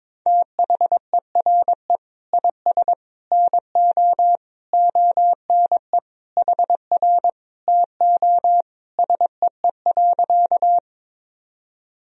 Morse code, heard as a keyed tone, about 21 wpm, 700 Hz. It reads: THERE IS NO ONE HR TO SEE.
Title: Sentences from Top 200 Words and Abbreviations 22wpm